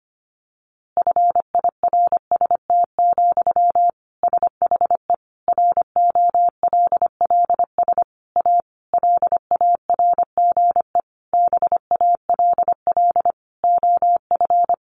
Morse code, heard as a keyed tone, about 25 wpm, 700 Hz. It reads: FIRST, SHE ROLLS A LARGE BALL OF
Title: Easy Phrases 25wpm